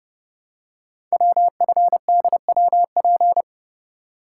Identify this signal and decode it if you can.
Morse code, heard as a keyed tone, about 30 wpm, 700 Hz. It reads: WFDWP
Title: Code Groups - 5 Letters - 30wpm